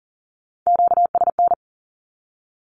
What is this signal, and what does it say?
Morse code, heard as a keyed tone, about 40 wpm, 700 Hz. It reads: QSN
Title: Q-Codes Repeated 3x 40wpm